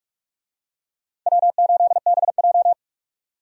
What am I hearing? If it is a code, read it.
Morse code, heard as a keyed tone, about 47 wpm, 700 Hz. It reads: W8BJ
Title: USA Call Signs - 4 Characters - 45wpm